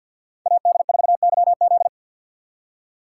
Morse code, heard as a keyed tone, about 53 wpm, 700 Hz. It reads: AD4YZ
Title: USA Call Signs - 6x Character Spacing - 50wpm